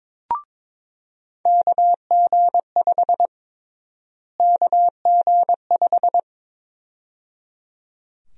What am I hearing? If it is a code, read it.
Morse code, heard as a keyed tone, about 22 wpm, 700 Hz. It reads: KG5 KG5
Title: USA Call Sign Prefixes - Repeated 2x - 22wpm